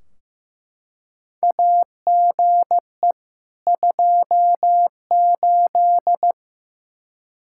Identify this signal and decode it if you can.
Morse code, heard as a keyed tone, about 15 wpm, 700 Hz. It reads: AGE 28
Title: Practice QSO - Exchange 3 - More Info - 15wpm